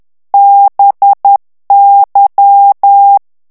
Morse code, heard as a keyed tone, about 10 wpm, 800 Hz. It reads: BY
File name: ANGLAIS texte clair 3 vitesse 10 mots minute